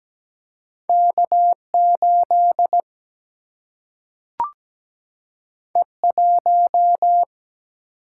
Morse code, heard as a keyed tone, about 17 wpm, 700 Hz. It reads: K8 E1
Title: Single-Letter Single-Number 17wpm